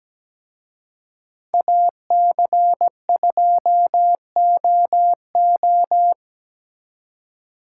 Morse code, heard as a keyed tone, about 20 wpm, 700 Hz. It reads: AC2OO
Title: Speed Racing - USA 2x2 Call Signs 17wpm